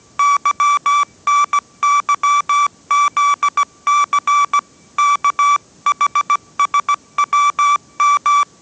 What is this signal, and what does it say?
Morse code, heard as a keyed tone, about 17 wpm, 1160 Hz. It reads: YNYZC KHSWM
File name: 18wpm